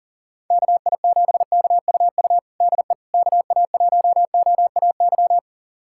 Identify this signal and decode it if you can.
Morse code, heard as a keyed tone, about 40 wpm, 700 Hz. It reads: KI7KUU DE KA1OAY
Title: USA Callsign DE Callsign 40wpm